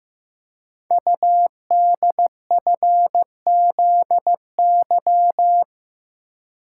Morse code, heard as a keyed tone, about 15 wpm, 700 Hz. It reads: UDFZY
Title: Code Groups - 5 Letters - 15wpm